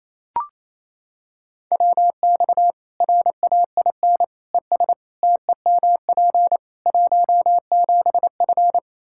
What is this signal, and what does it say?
Morse code, heard as a keyed tone, about 28 wpm, 700 Hz. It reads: WX RAIN ES TEMP 17F
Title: Practice QSO - Exchange 2 - Info - 28wpm